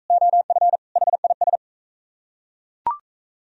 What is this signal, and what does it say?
Morse code, heard as a keyed tone, about 38 wpm, 700 Hz. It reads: OF HIS